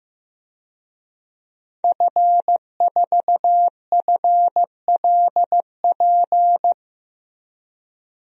Morse code, heard as a keyed tone, about 16 wpm, 700 Hz. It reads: F4FLP
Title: DXCC-100 Call Signs - 5 Characters - Repeated 3x - 15wpm